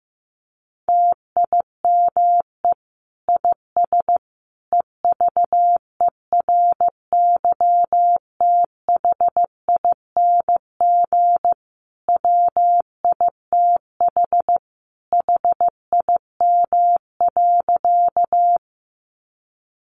Morse code, heard as a keyed tone, about 15 wpm, 700 Hz. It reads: TIME IS EVERYTHING WITH HIM.
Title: Sentences from Top 700 Words 15wpm